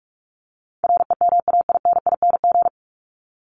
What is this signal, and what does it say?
Morse code, heard as a keyed tone, about 44 wpm, 700 Hz. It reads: REMAINING
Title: ING Suffix - Top 100 Words with Base Words 45wpm